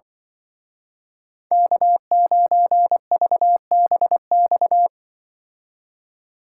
Morse code, heard as a keyed tone, about 24 wpm, 700 Hz. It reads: K9VBX
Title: Speed Racing - USA 1x3 Call Signs 20wpm